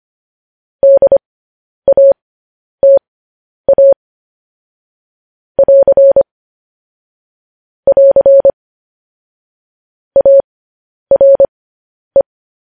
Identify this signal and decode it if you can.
Morse code, heard as a keyed tone, about 25 wpm, spaced as for 10 wpm, 550 Hz. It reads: DATA <AR> <AR> ARE
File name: podcast / News headlines-1x-25x10